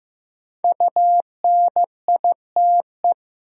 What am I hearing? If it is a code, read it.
Morse code, heard as a keyed tone, about 15 wpm, 700 Hz. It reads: UNITE
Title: Sentences from Top 1000 Words - Review Part 5 - New Word Drill - 15wpm